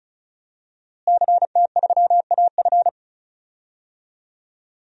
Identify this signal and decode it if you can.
Morse code, heard as a keyed tone, about 36 wpm, 700 Hz. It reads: CT3AF
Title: DXCC-100 Call Signs - 5 Characters - Repeated 2x - 35wpm